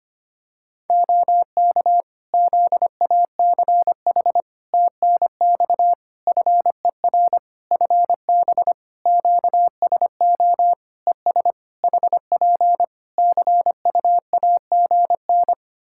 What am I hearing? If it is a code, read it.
Morse code, heard as a keyed tone, about 25 wpm, 700 Hz. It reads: OK ZACH TNX FER FB QSO ES HP CUAGN
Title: K1USN SST QSO Practice 25wpm